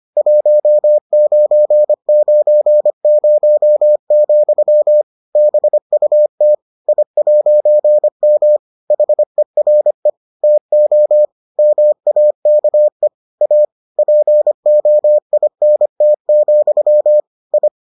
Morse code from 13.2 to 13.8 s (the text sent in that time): A